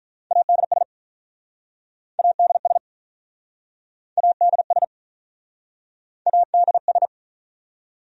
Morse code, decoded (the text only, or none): ADS ADS ADS ADS